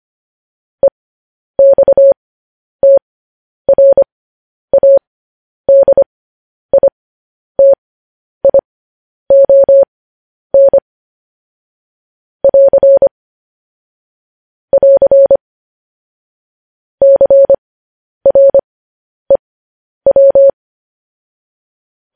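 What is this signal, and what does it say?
Morse code, heard as a keyed tone, about 25 wpm, spaced as for 10 wpm, 550 Hz. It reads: EXTRADITION <AR> <AR> CREW
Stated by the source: Morse code news headlines sent at 10.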